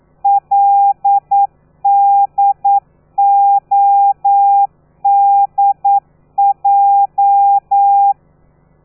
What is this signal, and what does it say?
Morse code, heard as a keyed tone, about 9 wpm, 800 Hz. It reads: LDODJ
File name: telegrafia / i2tfj / Lez13 40